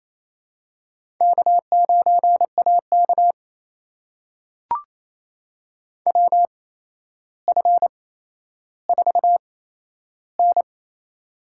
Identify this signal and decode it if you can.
Morse code, heard as a keyed tone, about 28 wpm, spaced as for 12 wpm, 700 Hz. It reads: K9AK WF4N
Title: USA Call Signs - 4 Characters - 8x Character Spacing - 28wpm